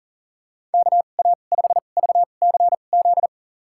Morse code, heard as a keyed tone, about 40 wpm, spaced as for 29 wpm, 700 Hz. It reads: KA5VCZ
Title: USA Call Signs - 2x Character Spacing - 40wpm